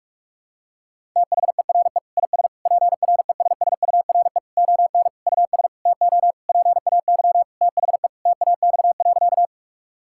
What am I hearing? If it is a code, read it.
Morse code, heard as a keyed tone, about 41 wpm, 700 Hz. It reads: THERE IS PRESSURE ON US TO PAY THE TAX.
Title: Sentences from Top 1000 Words - Review Part 6 - 45wpm